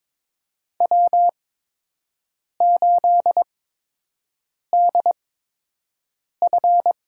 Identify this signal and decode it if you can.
Morse code, heard as a keyed tone, about 22 wpm, spaced as for 6 wpm, 700 Hz. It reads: W8DF
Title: USA Call Signs - 8x Character Spacing - 22wpm